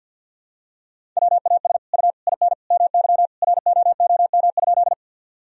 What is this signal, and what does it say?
Morse code, heard as a keyed tone, about 52 wpm, 700 Hz. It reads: WAS U IN MY ROOM?
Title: Sentences from Top 300 Words and Abbreviations 50wpm